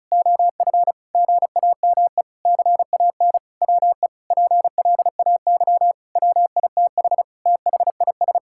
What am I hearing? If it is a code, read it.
Morse code, heard as a keyed tone, about 34 wpm, 700 Hz. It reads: OF GAME CAN WE PLAY WITH THIS
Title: Sentences from Top 1000 Words - Review Part 6 - 35wpm